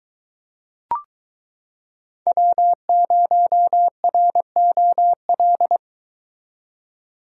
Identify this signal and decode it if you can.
Morse code, heard as a keyed tone, about 22 wpm, 700 Hz. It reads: W0ROL